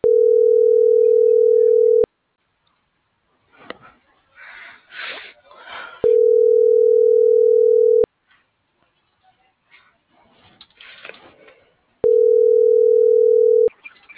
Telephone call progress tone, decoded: ringback tone

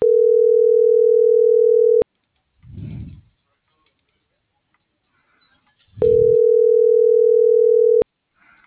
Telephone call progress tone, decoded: ringback tone

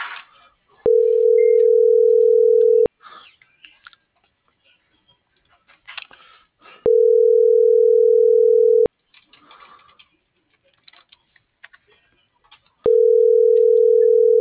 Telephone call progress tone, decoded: ringback tone

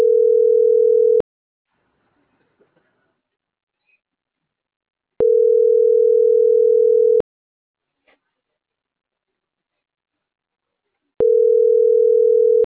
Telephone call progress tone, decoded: ringback tone

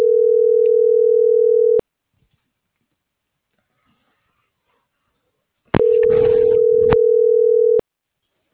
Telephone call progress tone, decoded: ringback tone

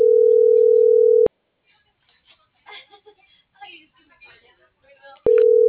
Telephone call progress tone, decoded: ringback tone

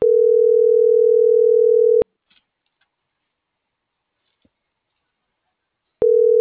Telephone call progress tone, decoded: ringback tone

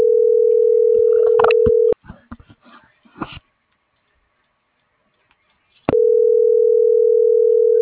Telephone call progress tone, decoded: ringback tone

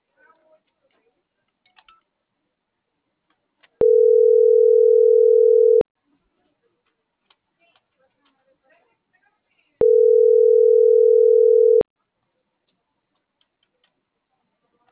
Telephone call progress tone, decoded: ringback tone

